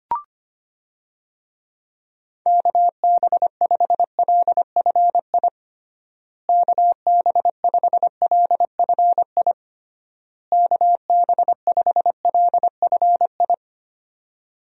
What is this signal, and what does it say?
Morse code, heard as a keyed tone, about 25 wpm, 700 Hz. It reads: KB5LFI KB5LFI KB5LFI
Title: USA Call Signs - 6 Characters - Repeated 3x - 25wpm